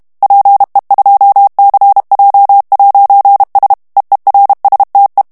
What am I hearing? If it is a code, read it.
Morse code, heard as a keyed tone, about 32 wpm, 800 Hz. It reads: PE2CJ'S EERSTE
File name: morse mp3 / HOLLANDAIS texte clair 2 vitesse 30 mots minute